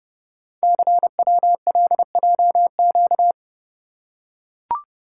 Morse code, heard as a keyed tone, about 28 wpm, 700 Hz. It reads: CWLJQ
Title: Code Groups - 5 Letters - 30wpm